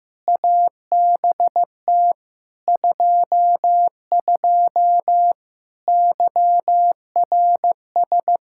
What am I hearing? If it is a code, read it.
Morse code, heard as a keyed tone, about 16 wpm, 700 Hz. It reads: ABT 22 YRS
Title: Practice QSO - Exchange 3 - More Info - 15wpm